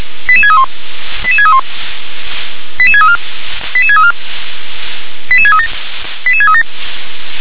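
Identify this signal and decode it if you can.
Five-tone selective call (ZVEI), heard as quick runs of five tones, tones of about 70 ms. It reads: 88631 88631 88634 88634 88637 88637
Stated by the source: Five-tone ZVEI selective call, received via SDR.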